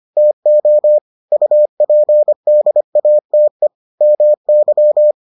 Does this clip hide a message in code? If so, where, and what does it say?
Yes: Morse code from 0.2 to 5.1 s, heard as a keyed tone, about 25 wpm, 600 Hz. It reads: TO UPDATE MY